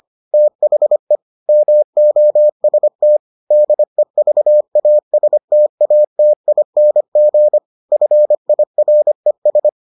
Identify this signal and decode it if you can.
Morse code, heard as a keyed tone, about 25 wpm, 600 Hz. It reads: THE MOST DEVASTATING FIRES